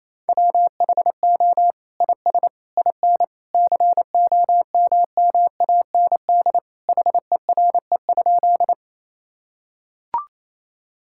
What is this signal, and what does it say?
Morse code, heard as a keyed tone, about 28 wpm, 700 Hz. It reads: WHO IS IN COMMAND HERE?
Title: Sentences from Top 1600 Words 28wpm